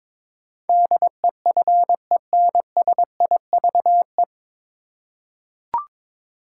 Morse code, heard as a keyed tone, about 22 wpm, 700 Hz. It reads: DEFENSIVE
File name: 9 Letter Words - First 3 Letters Given - 22wpm